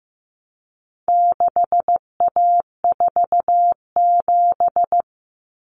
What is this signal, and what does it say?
Morse code, heard as a keyed tone, about 15 wpm, 700 Hz. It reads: 6A47
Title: Common Cut Numbers - 4-Digit 15wpm